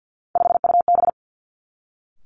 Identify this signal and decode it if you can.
Morse code, heard as a keyed tone, about 50 wpm, 700 Hz. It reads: 5UB
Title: Three Challenging Letter-Number Combinations 50wpm